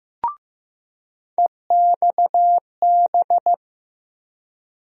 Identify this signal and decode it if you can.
Morse code, heard as a keyed tone, about 15 wpm, 700 Hz. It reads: EXB
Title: USA Call Sign Postfixes - 3 Letters 15wpm